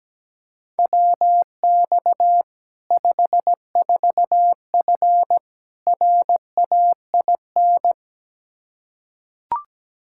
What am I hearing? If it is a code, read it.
Morse code, heard as a keyed tone, about 17 wpm, 700 Hz. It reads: WX 54F RAIN
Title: Practice QSO - Exchange 2 - Info - 17wpm